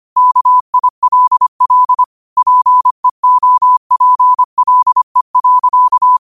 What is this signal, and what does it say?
Morse code, heard as a keyed tone, about 25 wpm, 1000 Hz. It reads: KILL PEOPLE.